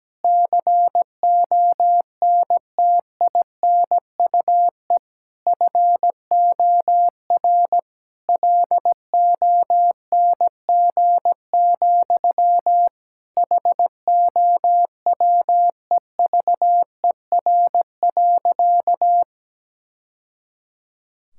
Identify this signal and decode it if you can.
Morse code, heard as a keyed tone, about 17 wpm, 700 Hz. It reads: CONTINUE FOR LONG, HOWEVER.
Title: Sentences from Top 1000 Words - Review Part 6 - 17wpm